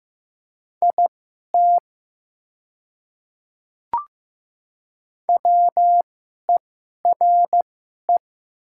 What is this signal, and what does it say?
Morse code, heard as a keyed tone, about 15 wpm, spaced as for 11 wpm, 700 Hz. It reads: IT WERE
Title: 2x Character Spacing - Top 100 Words - 15wpm